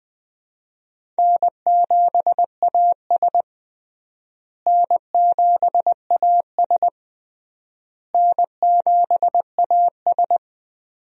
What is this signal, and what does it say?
Morse code, heard as a keyed tone, about 20 wpm, 700 Hz. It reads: N7AS N7AS N7AS
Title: USA Call Signs - 4 Characters - Repeated 3x - 20wpm